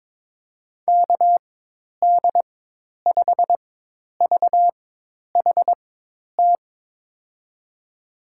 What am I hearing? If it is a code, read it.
Morse code, heard as a keyed tone, about 23 wpm, spaced as for 10 wpm, 700 Hz. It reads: KD5VHT
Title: USA Call Signs - 6 Characters - 4x Character Spacing - 22wpm